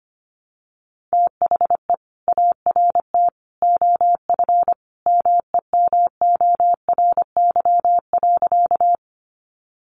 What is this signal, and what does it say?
Morse code, heard as a keyed tone, about 25 wpm, 700 Hz. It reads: THE ART OF MEMORY.
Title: Sentences from Top 1300 Words 25wpm